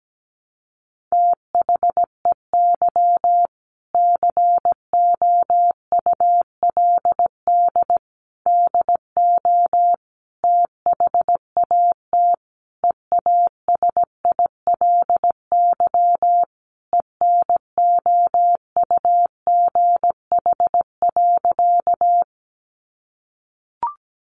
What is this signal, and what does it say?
Morse code, heard as a keyed tone, about 17 wpm, 700 Hz. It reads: THEY COULD DO THAT EASILY ENOUGH.
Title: Sentences from Top 900 Words 17wpm